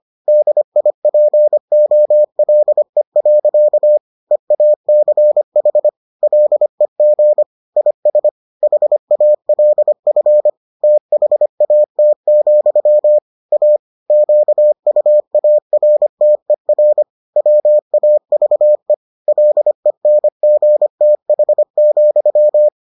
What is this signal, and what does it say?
Morse code, heard as a keyed tone, about 25 wpm, 600 Hz. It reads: DIPOLE. EACH LEG IS HALF THAT, A QUARTER WAVE LENGTH,